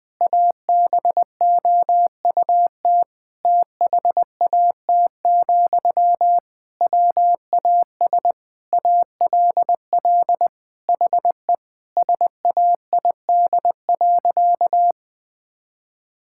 Morse code, heard as a keyed tone, about 20 wpm, 700 Hz. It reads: ABOUT THAT, WAS ALL HE SAID.
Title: Sentences from Top 1600 Words 20wpm